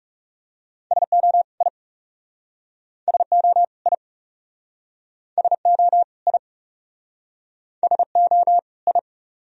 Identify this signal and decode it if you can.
Morse code, heard as a keyed tone, about 41 wpm, 700 Hz. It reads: SO I SO I SO I SO I